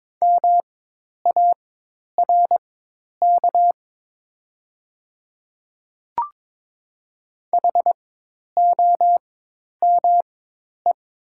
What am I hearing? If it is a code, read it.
Morse code, heard as a keyed tone, about 22 wpm, spaced as for 10 wpm, 700 Hz. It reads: MARK HOME
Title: 4x Character Spacing - Top 300 Words - 22wpm